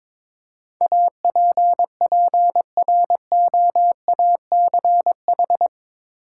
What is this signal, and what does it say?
Morse code, heard as a keyed tone, about 22 wpm, 700 Hz. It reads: APPROACH
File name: practice / 22wpm / Eight Letter Words - 22wpm